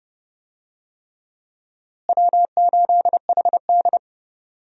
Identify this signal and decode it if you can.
Morse code, heard as a keyed tone, about 30 wpm, 700 Hz. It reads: W8HD
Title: USA Call Signs - 4 Characters - Repeated 3x - 30wpm